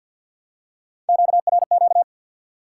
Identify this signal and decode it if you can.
Morse code, heard as a keyed tone, about 53 wpm, 700 Hz. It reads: QRQ
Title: Top 90 QSO Elements 50wpm